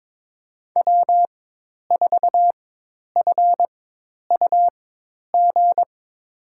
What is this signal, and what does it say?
Morse code, heard as a keyed tone, about 22 wpm, spaced as for 10 wpm, 700 Hz. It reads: W4FUG
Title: USA Call Signs - 5 Characters - 4x Character Spacing - 22wpm